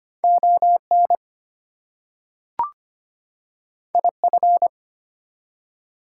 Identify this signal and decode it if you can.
Morse code, heard as a keyed tone, about 25 wpm, 700 Hz. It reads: ON IF
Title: Two Letter Words - 25wpm